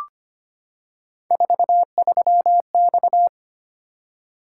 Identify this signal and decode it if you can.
Morse code, heard as a keyed tone, about 25 wpm, 700 Hz. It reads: 43X
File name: Three Letter-Number Combinations - 25wpm